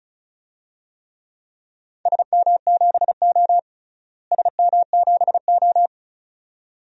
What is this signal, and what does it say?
Morse code, heard as a keyed tone, about 34 wpm, 700 Hz. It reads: SM7O SM7O
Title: DXCC-100 Call Signs - 4 Characters - Repeated 2x - 35wpm